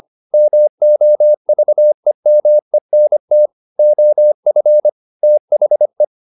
Morse code, heard as a keyed tone, about 25 wpm, 600 Hz. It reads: MOVEMENT OF THE
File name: the old man and the sea part 1 25wpm